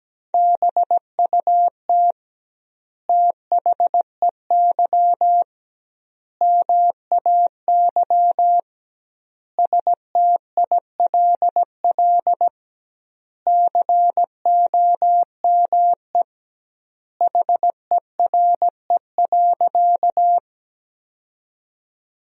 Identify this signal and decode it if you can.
Morse code, heard as a keyed tone, about 17 wpm, 700 Hz. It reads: BUT THEY MAY STILL COME HERE.
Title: 2x Word-Spacing - Sentences from Top 300 Words - 17wpm